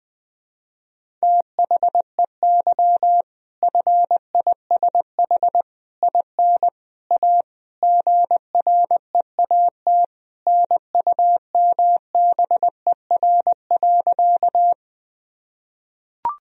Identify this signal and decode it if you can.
Morse code, heard as a keyed tone, about 20 wpm, 700 Hz. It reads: THEY FISH IN A GREAT NUMBER.
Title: Sentences from Top 1000 Words - Review Part 1 - 20wpm